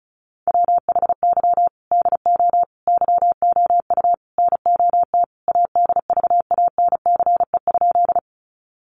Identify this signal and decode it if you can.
Morse code, heard as a keyed tone, about 34 wpm, 700 Hz. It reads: WHY DO YOU NOT ADVANCE?
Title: Sentences from Top 1400 Words 35wpm